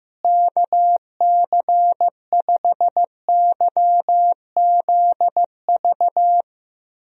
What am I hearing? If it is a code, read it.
Morse code, heard as a keyed tone, about 15 wpm, 700 Hz. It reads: KC5YZV
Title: USA Call Signs - 6 Characters - Repeated 3x - 15wpm